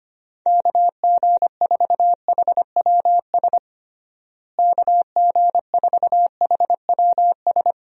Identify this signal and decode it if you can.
Morse code, heard as a keyed tone, about 25 wpm, 700 Hz. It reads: KG4HWS KG4HWS
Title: USA Call Signs - 6 Characters - Repeated 2x - 25wpm